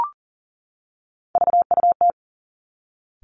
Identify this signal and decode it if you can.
Morse code, heard as a keyed tone, about 38 wpm, 690 Hz. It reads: VUT
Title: Three Challenging Letter-Number Combinations 40wpm